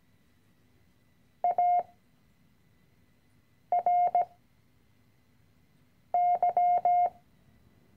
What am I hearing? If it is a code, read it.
Morse code, heard as a keyed tone, about 17 wpm, 710 Hz. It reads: A R Y
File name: CW-Online-Kurs / Anfänger / Lektion 010